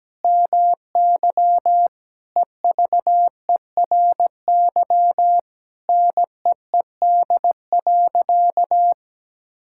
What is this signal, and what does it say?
Morse code, heard as a keyed tone, about 17 wpm, 700 Hz. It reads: MY EVERY NEED.
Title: Sentences from Top 1000 Words - Review Part 5 - 17wpm